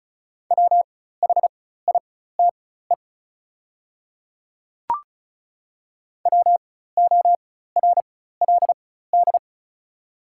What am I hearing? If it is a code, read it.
Morse code, heard as a keyed tone, about 36 wpm, spaced as for 16 wpm, 690 Hz. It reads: WHITE WORLD
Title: 4x Character Spacing - Top 300 Words - 35wpm